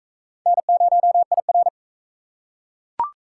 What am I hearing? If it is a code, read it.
Morse code, heard as a keyed tone, about 41 wpm, 700 Hz. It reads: N0IR